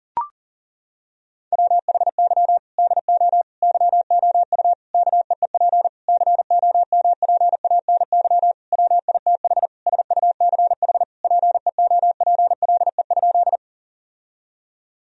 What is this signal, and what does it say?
Morse code, heard as a keyed tone, about 40 wpm, 700 Hz. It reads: WHY DO YOU KEEP COMPANY WITH SUCH PEOPLE?
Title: Sentences from Top 1000 Words - Review Part 3 - 40wpm